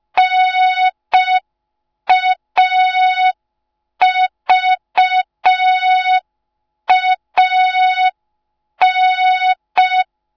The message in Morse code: NAVAN